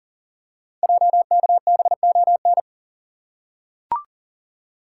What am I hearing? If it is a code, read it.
Morse code, heard as a keyed tone, about 39 wpm, 700 Hz. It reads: JKBON
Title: Code Groups - 5 Letter:Numbers - 40wpm